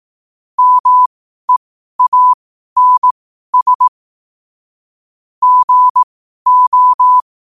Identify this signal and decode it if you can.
Morse code, heard as a keyed tone, about 18 wpm, spaced as for 12 wpm, 1000 Hz. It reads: MEANS GO
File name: qotdfiles / 12wpm